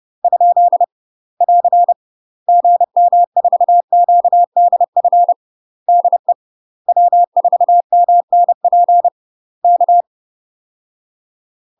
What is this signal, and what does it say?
Morse code, heard as a keyed tone, about 30 wpm, 700 Hz. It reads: ? <AR> GM4QDF DE W4MNP K
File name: UK MASTER QSOs TRAINING / 30WPM / UK MASTER QSOs 0014